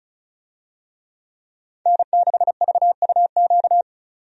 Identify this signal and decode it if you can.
Morse code, heard as a keyed tone, about 36 wpm, 700 Hz. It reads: N6VUQ
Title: USA Call Signs - 5 Characters - 35wpm